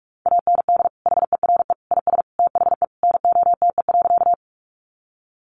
Morse code, heard as a keyed tone, about 44 wpm, 700 Hz. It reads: AND HERE IS THE NOTE.
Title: Sentences from Top 500 Words 45wpm